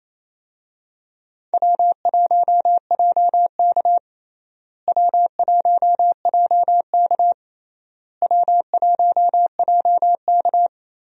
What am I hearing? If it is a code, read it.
Morse code, heard as a keyed tone, about 28 wpm, 700 Hz. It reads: W1JK W1JK W1JK
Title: USA Call Signs - 4 Characters - Repeated 3x - 28wpm